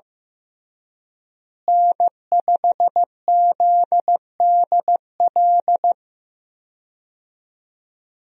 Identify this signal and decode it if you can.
Morse code, heard as a keyed tone, about 16 wpm, 700 Hz. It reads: N5ZDL
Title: USA Call Signs - 5 Characters - Repeated 3x - 15wpm